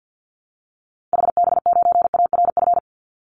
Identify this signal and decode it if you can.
Morse code, heard as a keyed tone, about 49 wpm, 700 Hz. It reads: HB9ARF
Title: Call Signs Repeated 3x - 50wpm